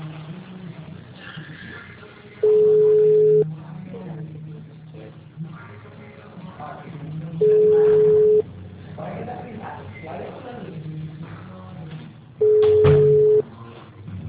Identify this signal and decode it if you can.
Ringback tone (Europe, 425 Hz), with speech.